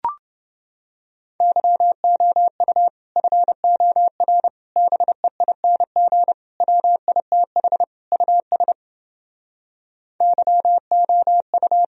Morse code, heard as a keyed tone, about 30 wpm, 700 Hz. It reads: YOU FOR BEING WITH US YOU